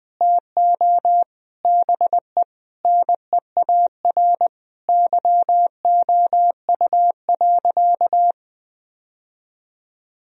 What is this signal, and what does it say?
Morse code, heard as a keyed tone, about 20 wpm, 700 Hz. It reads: TO BE NEAR YOU.
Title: Sentences from Top 1000 Words - Review Part 1 - 20wpm